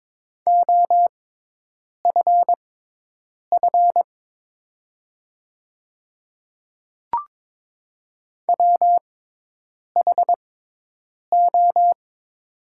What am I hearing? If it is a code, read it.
Morse code, heard as a keyed tone, about 22 wpm, spaced as for 8 wpm, 690 Hz. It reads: OFF WHO
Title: 6x Character Spacing - Top 500 Words - 22wpm